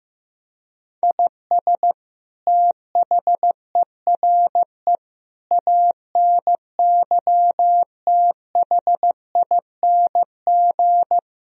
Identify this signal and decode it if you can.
Morse code, heard as a keyed tone, about 15 wpm, 700 Hz. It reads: IS THERE ANYTHING
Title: Sentences from Top 1000 Words - Review Part 4 - 15wpm